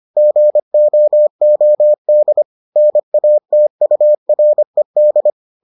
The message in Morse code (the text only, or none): GOOD NATURED